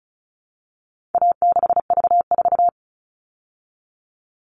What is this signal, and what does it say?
Morse code, heard as a keyed tone, about 34 wpm, 700 Hz. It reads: A6V4
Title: All Cut Numbers - 4-Digit 35wpm